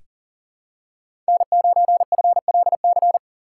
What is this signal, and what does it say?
Morse code, heard as a keyed tone, about 40 wpm, 700 Hz. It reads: N9FLC
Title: USA Call Signs - 5 Characters - 2x Character Spacing - 40wpm